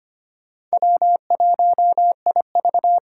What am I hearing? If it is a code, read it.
Morse code, heard as a keyed tone, about 25 wpm, 700 Hz. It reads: W1IV
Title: Four Challenging Letter-Number Combinations 25wpm